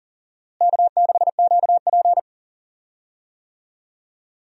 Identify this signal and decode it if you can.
Morse code, heard as a keyed tone, about 37 wpm, 700 Hz. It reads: K6QP
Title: USA Call Signs - 4 Characters - Repeated 2x - 40wpm